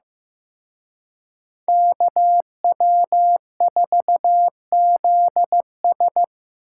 Morse code, heard as a keyed tone, about 18 wpm, 700 Hz. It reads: KW4ZS
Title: Speed Racing - USA Call Signs 15wpm